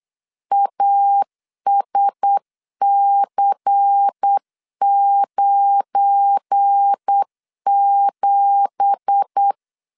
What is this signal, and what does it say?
Morse code, heard as a keyed tone, about 8 wpm, 800 Hz. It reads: ASC97
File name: 8wpm